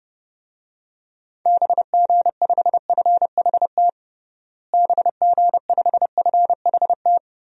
Morse code, heard as a keyed tone, about 30 wpm, 700 Hz. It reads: BG5FHT BG5FHT
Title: DXCC-100 Call Signs - 6 Characters - Repeated 3x - 30wpm